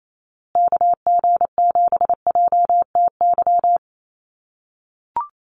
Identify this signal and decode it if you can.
Morse code, heard as a keyed tone, about 28 wpm, 700 Hz. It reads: KG7JTY
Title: USA Call Signs Repeated 2x 28wpm